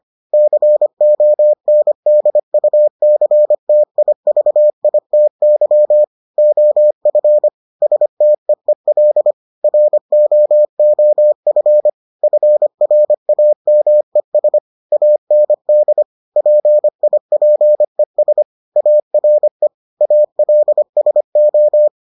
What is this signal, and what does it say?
Morse code, heard as a keyed tone, about 25 wpm, 600 Hz. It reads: CONDUCTIVITY OF STEEL ROOF FRAMES AND PIPES ARE ALSO